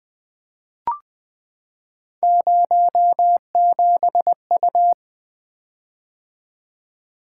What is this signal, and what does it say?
Morse code, heard as a keyed tone, about 20 wpm, 700 Hz. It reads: 07U
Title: Three Letter-Number Combinations - 20wpm